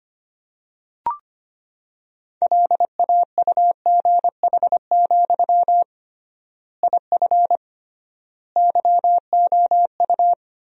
Morse code, heard as a keyed tone, about 25 wpm, 700 Hz. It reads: LAUGH, IF YOU
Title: 3x Word-Spacing - Sentences from Top 500 Words - 25wpm